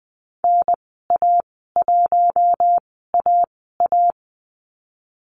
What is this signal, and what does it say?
Morse code, heard as a keyed tone, about 20 wpm, spaced as for 15 wpm, 700 Hz. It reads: NA1AA
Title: 055 Lesson 10g - Letter r - Call Signs 20wpm